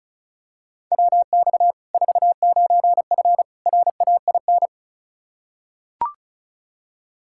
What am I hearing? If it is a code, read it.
Morse code, heard as a keyed tone, about 36 wpm, 700 Hz. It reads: WX 49F RAIN
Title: Practice QSO - Exchange 2 - Info - 35wpm